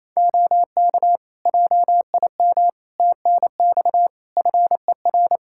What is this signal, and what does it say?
Morse code, heard as a keyed tone, about 28 wpm, 700 Hz. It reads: OK JIM TNX FER
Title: Practice QSO - Exchange 4 - Ending - 28wpm